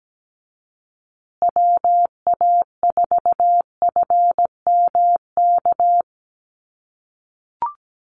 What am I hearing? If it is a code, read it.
Morse code, heard as a keyed tone, about 17 wpm, 700 Hz. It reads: WA4FMK
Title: USA Call Signs Repeated 3x 17wpm